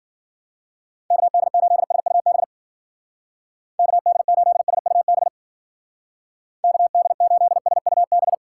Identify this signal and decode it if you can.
Morse code, heard as a keyed tone, about 50 wpm, 700 Hz. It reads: KD8SUB KD8SUB KD8SUB
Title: Speed Racing - USA Call Signs 40wpm